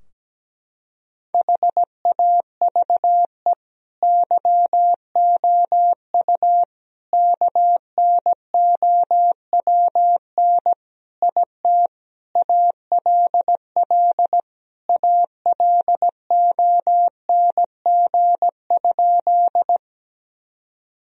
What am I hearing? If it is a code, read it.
Morse code, heard as a keyed tone, about 17 wpm, 700 Hz. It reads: HAVE YOU KNOWN IT ALL ALONG?
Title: Sentences from Top 1000 Words - Review Part 3 - 17wpm